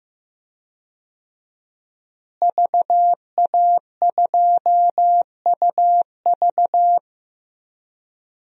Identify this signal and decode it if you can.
Morse code, heard as a keyed tone, about 16 wpm, 700 Hz. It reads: VA2UV
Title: DXCC-100 Call Signs - 5 Characters - Repeated 2x - 15wpm